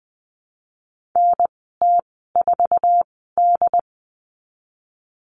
Morse code, heard as a keyed tone, about 20 wpm, spaced as for 15 wpm, 700 Hz. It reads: NT4D
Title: 076 Lesson 13g - Letter L - Call Signs 20wpm